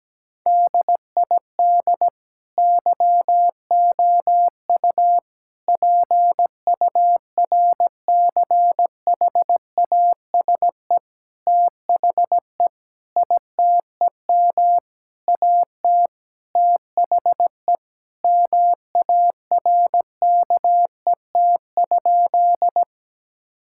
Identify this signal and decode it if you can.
Morse code, heard as a keyed tone, about 17 wpm, 700 Hz. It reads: DID YOU PURCHASE THE ITEM AT THE MARKET?
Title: Sentences from Top 1000 Words - Review Part 6 - 17wpm